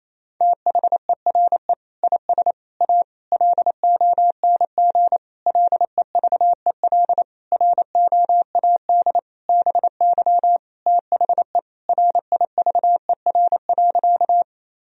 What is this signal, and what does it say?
Morse code, heard as a keyed tone, about 28 wpm, 700 Hz. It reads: THERE IS A LONG LEVEL ROAD BY THE RIVER.
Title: Sentences from Top 1000 Words - Review Part 4 - 28wpm